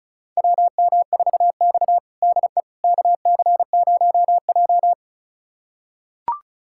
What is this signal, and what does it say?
Morse code, heard as a keyed tone, about 36 wpm, 700 Hz. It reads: WM4X DE KC0J
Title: USA Callsign DE Callsign - 4 Characters - 35wpm